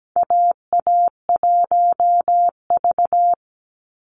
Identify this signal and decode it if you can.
Morse code, heard as a keyed tone, about 17 wpm, 700 Hz. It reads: AA1V
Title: Call Signs Repeated 2x - 17wpm